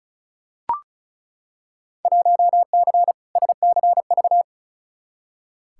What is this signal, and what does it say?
Morse code, heard as a keyed tone, about 36 wpm, 700 Hz. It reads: 1C SCV
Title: Field Day Exchanges - 35wpm